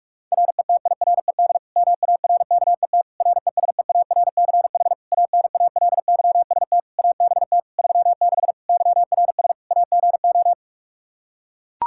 Morse code, heard as a keyed tone, about 46 wpm, 700 Hz. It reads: RETIRED MARKET RESEARCH ANALYST ABT 36 YRS AGO
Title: Practice QSO - Exchange 3 - More Info - 45wpm